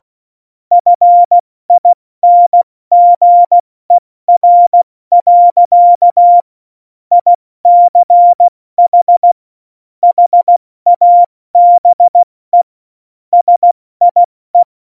Morse code, heard as a keyed tone, about 16 wpm, 700 Hz. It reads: FINGER. ICH HABE SIE
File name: Morsekurs / Witze / Witz 01. Dezember 2020, 70 F80